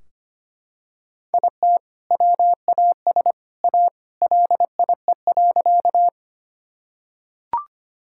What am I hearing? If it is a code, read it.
Morse code, heard as a keyed tone, about 25 wpm, 700 Hz. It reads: IT WAS A LIE.
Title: Sentences from Top 1600 Words 25wpm